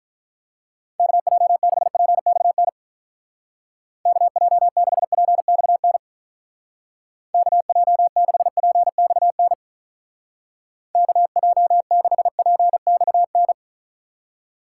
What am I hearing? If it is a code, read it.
Morse code, heard as a keyed tone, about 44 wpm, 700 Hz. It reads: KJ6PXN KJ6PXN KJ6PXN KJ6PXN